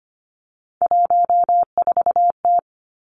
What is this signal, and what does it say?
Morse code, heard as a keyed tone, about 25 wpm, 700 Hz. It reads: 14T
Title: Three Challenging Letter-Number Combinations 25wpm